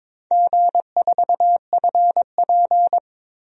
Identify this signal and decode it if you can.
Morse code, heard as a keyed tone, about 22 wpm, 700 Hz. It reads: G4FP
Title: DXCC-100 Call Signs - 4 Characters - Repeated 3x - 22wpm